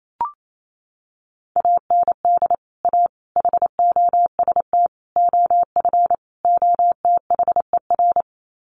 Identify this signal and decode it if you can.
Morse code, heard as a keyed tone, about 28 wpm, 700 Hz. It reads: AND A HOST OF OTHER
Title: Sets of 5 Words Encore2 28wpm